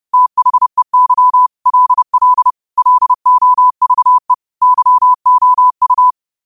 Morse code, heard as a keyed tone, about 30 wpm, 1000 Hz. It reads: THEY LL LOVE YOU
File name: qotdfiles / 30wpm